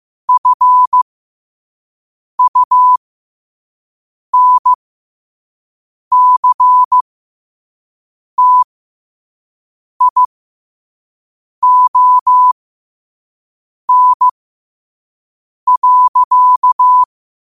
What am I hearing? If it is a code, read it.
Morse code, heard as a keyed tone, about 15 wpm, spaced as for 5 wpm, 1000 Hz. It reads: FUNCTION.
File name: qotdfiles / 5wpm